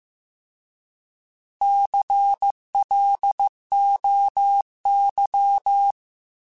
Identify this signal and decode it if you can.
Morse code, heard as a keyed tone, about 15 wpm, 800 Hz. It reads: CLOY